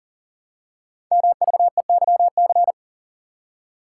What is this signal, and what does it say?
Morse code, heard as a keyed tone, about 42 wpm, 700 Hz. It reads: MVEYC
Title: Code Groups - 5 Letters - 40wpm